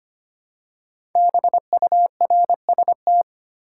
Morse code, heard as a keyed tone, about 25 wpm, 700 Hz. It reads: BURST
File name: 5 Letter Words - First 3 Letters Given - 25wpm